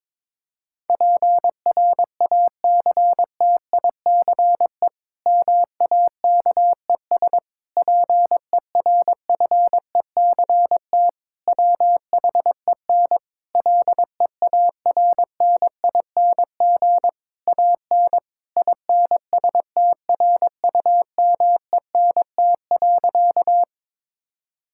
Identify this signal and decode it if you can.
Morse code, heard as a keyed tone, about 22 wpm, 700 Hz. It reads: PRACTICE MAKES PERFECT WHEN LEARNING AN INSTRUMENT.
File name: Difficult Sentences with Hints - 22wpm